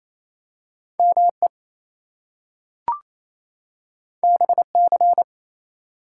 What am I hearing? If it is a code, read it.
Morse code, heard as a keyed tone, about 29 wpm, 700 Hz. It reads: ME BC
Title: US States and Canadian Provinces 28wpm